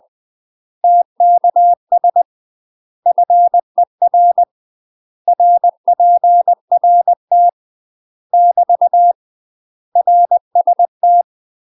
Morse code, heard as a keyed tone, about 20 wpm, 700 Hz. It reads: TKS FER RPRT <BT> RST
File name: UK MASTER QSOs TRAINING / 20WPM / UK MASTER QSOs 0068